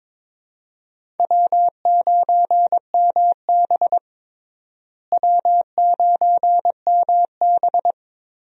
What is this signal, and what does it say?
Morse code, heard as a keyed tone, about 22 wpm, 700 Hz. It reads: W9MB W9MB
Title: USA Call Signs - 4 Characters - Repeated 3x - 22wpm